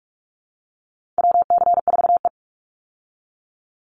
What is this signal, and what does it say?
Morse code, heard as a keyed tone, about 44 wpm, 700 Hz. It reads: WC4E
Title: Call Signs Repeated 2x - 45wpm